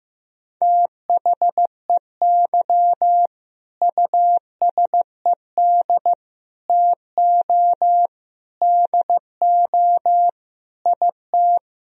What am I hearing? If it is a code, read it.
Morse code, heard as a keyed tone, about 16 wpm, 700 Hz. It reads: THEY USED TO DO IT
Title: Sentences from Top 1000 Words - Review Part 4 - 15wpm